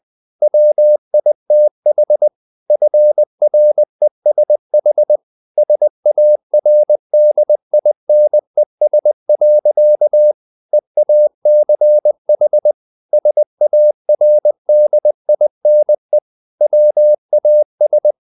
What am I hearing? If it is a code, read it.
Morse code, heard as a keyed tone, about 20 wpm, 600 Hz. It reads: WITH FRESH SARDINES. EACH SARDINE WAS
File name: the old man and the sea / the old man and the sea part 1 20wpm